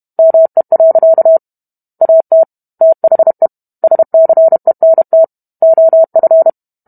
Morse code, heard as a keyed tone, about 32 wpm, 650 Hz. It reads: ME. AT THE SCENT OF